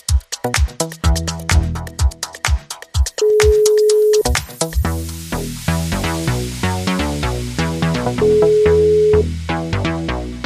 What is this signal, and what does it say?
Ringback tone (Europe, 425 Hz).